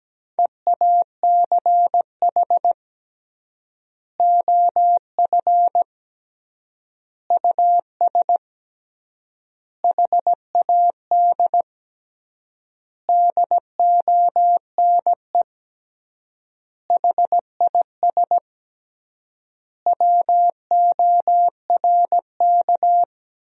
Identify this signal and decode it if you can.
Morse code, heard as a keyed tone, about 17 wpm, 700 Hz. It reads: EACH OF US HAD DONE HIS WORK
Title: 3x Word-Spacing - Sentences from Top 500 Words - 17wpm